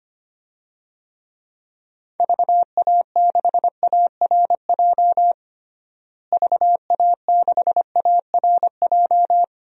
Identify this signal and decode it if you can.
Morse code, heard as a keyed tone, about 25 wpm, 700 Hz. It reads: VA6ARJ VA6ARJ
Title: DXCC-100 Call Signs - 6 Characters - Repeated 2x - 25wpm